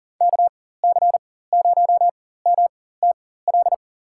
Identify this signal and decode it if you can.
Morse code, heard as a keyed tone, about 40 wpm, spaced as for 19 wpm, 700 Hz. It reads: KC0MTL
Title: USA Call Signs - 6 Characters - 4x Character Spacing - 40wpm